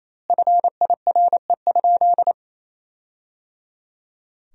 Morse code, heard as a keyed tone, about 28 wpm, 700 Hz. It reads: FIRE?
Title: 2x Word-Spacing - Sentences from Top 400 Words - 28wpm